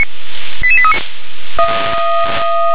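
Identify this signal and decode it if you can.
Five-tone selective call (ZVEI), heard as quick runs of five tones, tones of about 70 ms. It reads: 88839
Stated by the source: ZVEI five-tone selective call received on an SDR.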